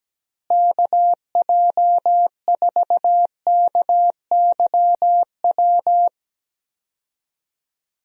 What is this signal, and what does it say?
Morse code, heard as a keyed tone, about 17 wpm, 700 Hz. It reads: KJ4KYW
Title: USA Call Signs - 6 Characters - Repeated 3x - 17wpm